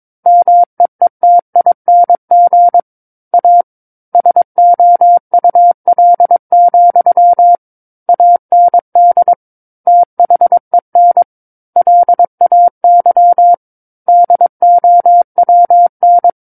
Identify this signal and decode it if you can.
Morse code, heard as a keyed tone, about 22 wpm, 700 Hz. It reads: MEETING A SOUL, AND THEN LAY DOWN
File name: Book One / 13-Chapter 13 - 22 WPM